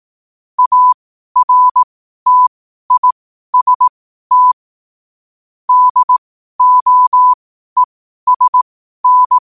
Morse code, heard as a keyed tone, about 18 wpm, spaced as for 12 wpm, 1000 Hz. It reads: ARTIST DOESN
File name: qotdfiles / 12wpm